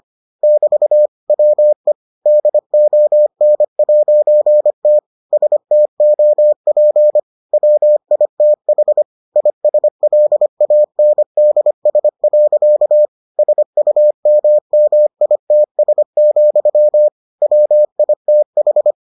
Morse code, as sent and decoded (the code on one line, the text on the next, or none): -...- / .-- . / -.. --- -. .----. - / ... - --- .--. / .-- .. - .... / .. ... .-.. .- -. -.. ... .-.-.- / ... ..- -- -- .. - ... --..-- / .-- .. - ....
<BT> WE DON'T STOP WITH ISLANDS. SUMMITS, WITH